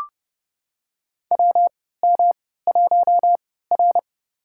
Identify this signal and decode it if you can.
Morse code, heard as a keyed tone, about 32 wpm, spaced as for 17 wpm, 700 Hz. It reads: WM1R
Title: USA Call Signs - 4 Characters - 3x Character Spacing - 30wpm